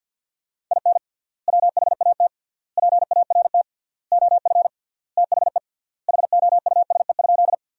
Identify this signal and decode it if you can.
Morse code, heard as a keyed tone, about 50 wpm, 700 Hz. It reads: IN WHAT PART OF THE HOUSE?
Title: 3x Word-Spacing - Sentences from Top 200 Words - 50wpm